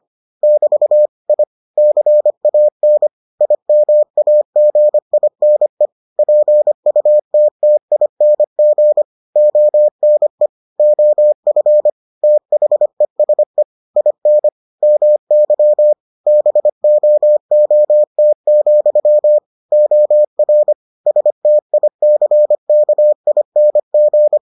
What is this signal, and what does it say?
Morse code, heard as a keyed tone, about 25 wpm, 600 Hz. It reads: <BT> I CAN IMAGINE PUTTING ONE OF THESE IN MY BOOT, OR STICKING